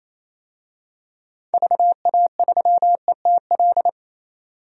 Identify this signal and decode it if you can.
Morse code, heard as a keyed tone, about 28 wpm, 700 Hz. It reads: VA3ETL
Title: DXCC-100 Call Signs - 6 Characters - 28wpm